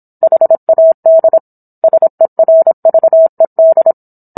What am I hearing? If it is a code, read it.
Morse code, heard as a keyed tone, about 27 wpm, 650 Hz. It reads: HAD SERVED